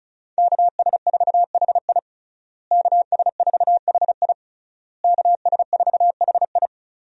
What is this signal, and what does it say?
Morse code, heard as a keyed tone, about 36 wpm, 700 Hz. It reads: KS4HI KS4HI KS4HI
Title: USA Call Signs - 5 Characters - Repeated 3x - 35wpm